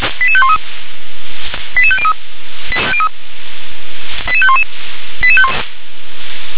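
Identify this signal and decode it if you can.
Five-tone selective call (ZVEI), heard as quick runs of five tones, tones of about 70 ms. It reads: 88513 88503 88511
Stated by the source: ZVEI five-tone call received by SDR.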